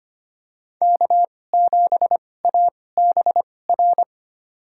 Morse code, heard as a keyed tone, about 25 wpm, spaced as for 18 wpm, 700 Hz. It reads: K7ABR
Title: USA Call Signs - 5 Characters - 2x Character Spacing - 25wpm